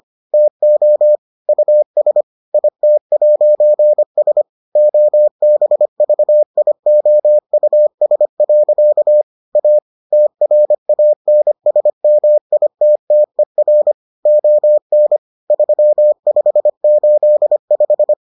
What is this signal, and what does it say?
Morse code, heard as a keyed tone, about 25 wpm, 600 Hz. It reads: TO US IT'S OBVIOUS. A TRANSMITTER ON 3585